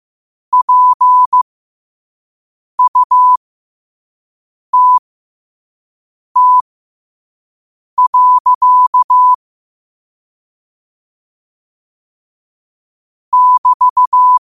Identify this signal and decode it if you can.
Morse code, heard as a keyed tone, about 15 wpm, spaced as for 5 wpm, 1000 Hz. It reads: PUTT. <BT>
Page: New quotes every day in morse code at 5 Words per minute.